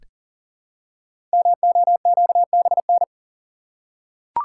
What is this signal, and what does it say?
Morse code, heard as a keyed tone, about 40 wpm, 700 Hz. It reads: MOQBN
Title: Code Groups - 5 Letters - 40wpm